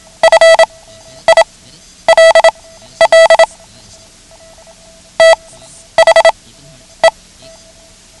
Morse code, heard as a keyed tone, about 27 wpm, spaced as for 11 wpm, 650 Hz. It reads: FILL THE
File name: ham radio / CodeQuick / Code Quick 09